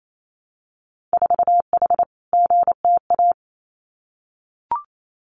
Morse code, heard as a keyed tone, about 28 wpm, 690 Hz. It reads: 4H GTA
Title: Winter Field Day Exchanges Encore 28wpm